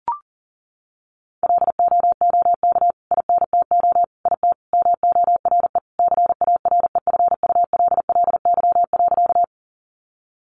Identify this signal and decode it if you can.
Morse code, heard as a keyed tone, about 40 wpm, 700 Hz. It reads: LOOK INTO IT MORE CAREFULLY.
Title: Sentences from Top 1300 Words 40wpm